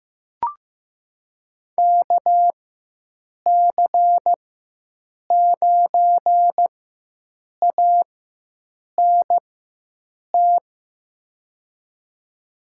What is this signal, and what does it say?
Morse code, heard as a keyed tone, about 15 wpm, spaced as for 7 wpm, 700 Hz. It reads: KC9ANT
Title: USA Call Signs - 4x Character Spacing - 15wpm